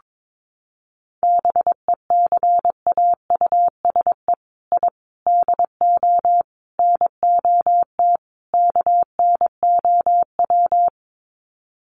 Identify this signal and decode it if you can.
Morse code, heard as a keyed tone, about 22 wpm, 700 Hz. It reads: BECAUSE I DO NOT KNOW
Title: Sets of 5 Words Encore2 22wpm